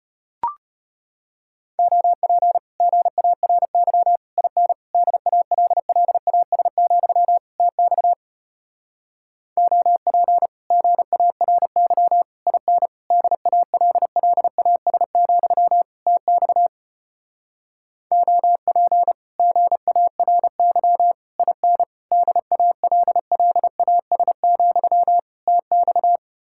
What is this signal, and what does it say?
Morse code, heard as a keyed tone, about 33 wpm, 700 Hz. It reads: OP GARY IN DALLAS, TX OP GARY IN DALLAS, TX OP GARY IN DALLAS, TX